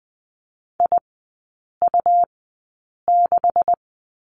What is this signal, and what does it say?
Morse code, heard as a keyed tone, about 20 wpm, spaced as for 8 wpm, 700 Hz. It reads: IU6
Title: 127 Lesson 21e - Number 6 - Three Characters 20wpm